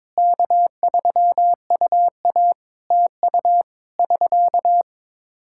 Morse code, heard as a keyed tone, about 22 wpm, 700 Hz. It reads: K3UA TU <SK>
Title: Practice QSO - Exchange 4 - Ending - 22wpm